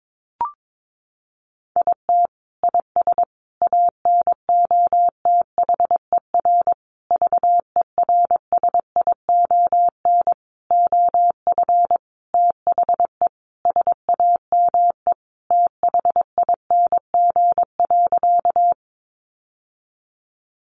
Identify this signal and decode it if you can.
Morse code, heard as a keyed tone, about 22 wpm, 700 Hz. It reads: IT IS ANOTHER VERSION OF THE SAME THING.
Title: Sentences from Top 1400 Words 22wpm